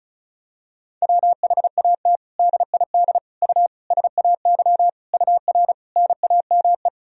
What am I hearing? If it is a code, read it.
Morse code, heard as a keyed tone, about 34 wpm, 700 Hz. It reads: WHAT DID U SAY UR NAME
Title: Sentences from Top 200 Words and Abbreviations 35wpm